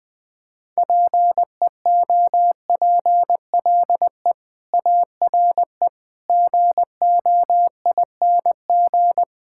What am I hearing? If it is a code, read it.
Morse code, heard as a keyed tone, about 21 wpm, 700 Hz. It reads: PEOPLE ARE GOING
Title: Build A Phrase - 4-Word Phrases - 20wpm